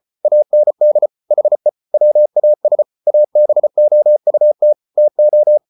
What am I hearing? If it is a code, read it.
Morse code, heard as a keyed tone, about 34 wpm, 600 Hz. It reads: AND HE WAS ABOUT TO